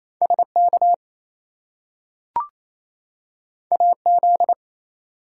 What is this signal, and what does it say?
Morse code, heard as a keyed tone, about 28 wpm, 700 Hz. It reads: SK AZ
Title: US States and Canadian Provinces 28wpm